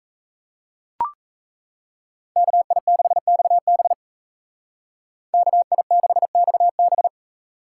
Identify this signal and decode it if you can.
Morse code, heard as a keyed tone, about 36 wpm, 700 Hz. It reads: KI6XB KI6XB